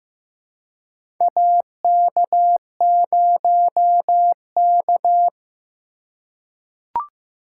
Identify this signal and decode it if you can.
Morse code, heard as a keyed tone, about 15 wpm, 700 Hz. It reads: AK0K
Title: USA Call Signs - 4 Characters - 4x Character Spacing - 15wpm